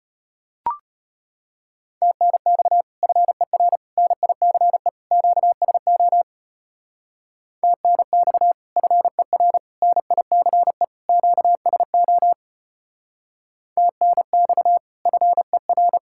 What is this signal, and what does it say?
Morse code, heard as a keyed tone, about 30 wpm, 700 Hz. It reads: TNX FER NICE QSO TNX FER NICE QSO TNX FER
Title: Speed-Racing Common QSO Phrases - 25wpm